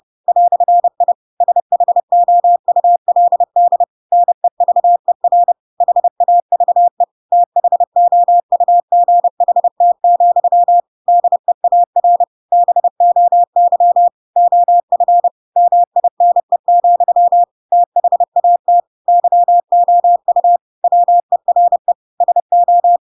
Morse code, heard as a keyed tone, about 30 wpm, 700 Hz. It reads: "I SHOULD NEVER HAVE THOUGHT, DEAR BOY OF MINE, THAT YOU WERE SO
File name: Pinocchio / Pinocchio ch7 30 30 ik1hge